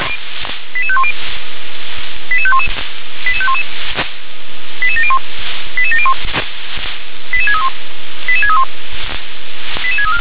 Five-tone selective call (ZVEI), heard as quick runs of five tones, tones of about 70 ms. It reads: 88510 88511 88511 88701 88701 88631 88631 88634